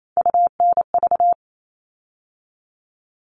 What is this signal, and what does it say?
Morse code, heard as a keyed tone, about 28 wpm, 700 Hz. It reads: UNV
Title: All Cut Numbers - 3-Digit 28wpm